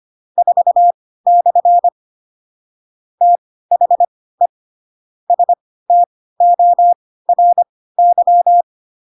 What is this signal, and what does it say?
Morse code, heard as a keyed tone, about 25 wpm, spaced as for 16 wpm, 700 Hz. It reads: 4/ THE STORY
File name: Pinocchio / Pinocchio ch4 10 25 ik1hge